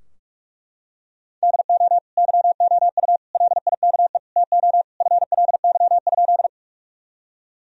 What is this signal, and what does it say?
Morse code, heard as a keyed tone, about 41 wpm, 700 Hz. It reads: DO YOU LIKE TO FLY?
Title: Sentences from Top 1000 Words - Review Part 2 - 45wpm